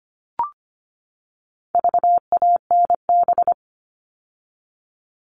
Morse code, heard as a keyed tone, about 25 wpm, 700 Hz. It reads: VANB
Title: All Cut Numbers - 4-Digit 25wpm